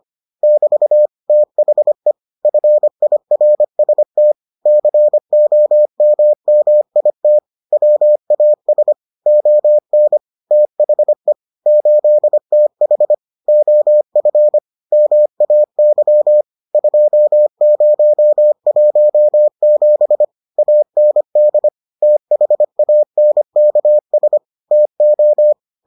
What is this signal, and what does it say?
Morse code, heard as a keyed tone, about 25 wpm, 600 Hz. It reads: <BT> THE FIRST COMMIT WAS ON THE 8TH OF MAY 2017 AND THANKS TO